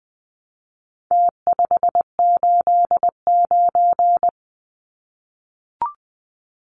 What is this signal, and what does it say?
Morse code, heard as a keyed tone, about 20 wpm, 700 Hz. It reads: T589
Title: Numbers - Contest Style - 4-Digit 20wpm